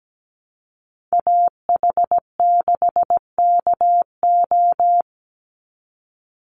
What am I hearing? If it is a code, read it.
Morse code, heard as a keyed tone, about 17 wpm, 700 Hz. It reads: AH6KO
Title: Call Signs Repeated 3x - 17wpm